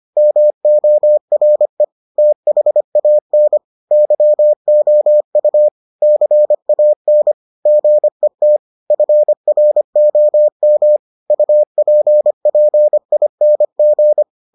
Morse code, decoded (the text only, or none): MORE THAN YOU CAN GET FROM UPPING